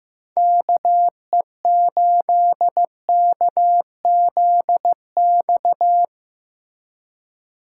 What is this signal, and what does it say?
Morse code, heard as a keyed tone, about 17 wpm, 700 Hz. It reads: KE8KZX
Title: Speed Racing - USA 2x3 Call Signs 15wpm